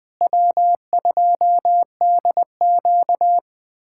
Morse code, heard as a keyed tone, about 20 wpm, 700 Hz. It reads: W2DQ
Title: USA Callsign DE Callsign - 4 Characters - 20wpm